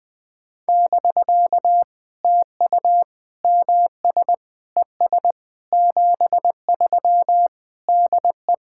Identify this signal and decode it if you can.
Morse code, heard as a keyed tone, about 20 wpm, 700 Hz. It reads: <BK> TU MS ES 73 DE
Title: POTA QSO Practice - 20wpm